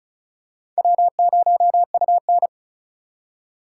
Morse code, heard as a keyed tone, about 36 wpm, 700 Hz. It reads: W0UN
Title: USA Call Signs - 4 Characters - 3x Character Spacing - 35wpm